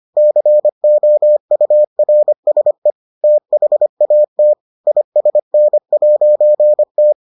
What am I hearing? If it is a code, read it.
Morse code, heard as a keyed tone, about 25 wpm, 600 Hz. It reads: COURSE THAT ISN'T